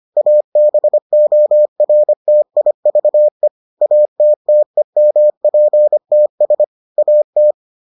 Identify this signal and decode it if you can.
Morse code, heard as a keyed tone, about 25 wpm, 600 Hz. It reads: ABORTIVE ATTEMPTS AT